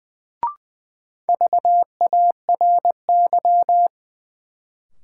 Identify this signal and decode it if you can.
Morse code, heard as a keyed tone, about 20 wpm, 700 Hz. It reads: VARY
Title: Sentences from Top 1200 Words - New Word Drill Alt - 20wpm